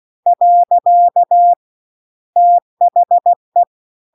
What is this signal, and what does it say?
Morse code, heard as a keyed tone, about 16 wpm, 700 Hz. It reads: . THE